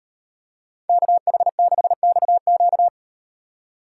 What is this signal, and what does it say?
Morse code, heard as a keyed tone, about 35 wpm, 700 Hz. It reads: KH6XQ